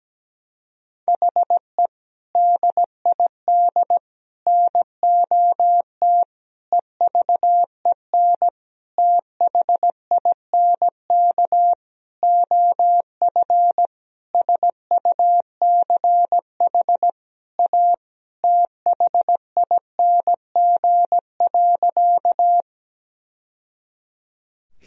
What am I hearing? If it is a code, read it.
Morse code, heard as a keyed tone, about 17 wpm, 700 Hz. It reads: HE DID NOT EVEN THINK OF SUCH A THING.
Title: Sentences from Top 1000 Words - Review Part 1 - 17wpm